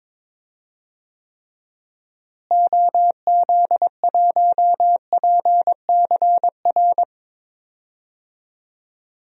Morse code, heard as a keyed tone, about 22 wpm, 700 Hz. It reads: OZ1PCR
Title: DXCC-100 Call Signs - 6 Characters - Repeated 2x - 22wpm